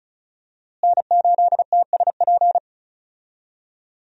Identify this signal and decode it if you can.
Morse code, heard as a keyed tone, about 34 wpm, 700 Hz. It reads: N8TSP
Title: USA Call Signs - 5 Characters - Repeated 2x - 35wpm